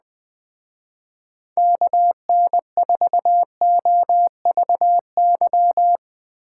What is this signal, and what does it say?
Morse code, heard as a keyed tone, about 21 wpm, 700 Hz. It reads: KN4OVY